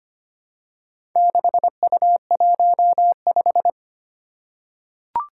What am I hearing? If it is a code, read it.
Morse code, heard as a keyed tone, about 25 wpm, 700 Hz. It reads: 6U15
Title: Four Challenging Letter-Number Combinations 25wpm